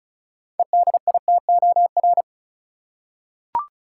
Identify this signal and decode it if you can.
Morse code, heard as a keyed tone, about 34 wpm, 700 Hz. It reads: EDITOR
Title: Six Letter Words - 35wpm